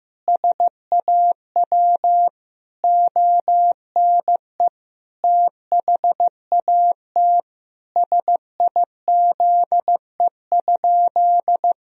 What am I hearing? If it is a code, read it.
Morse code, heard as a keyed tone, about 16 wpm, 700 Hz. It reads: SAW ONE THAT SIZE?
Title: Sentences from Top 1000 Words - Review Part 2 - 15wpm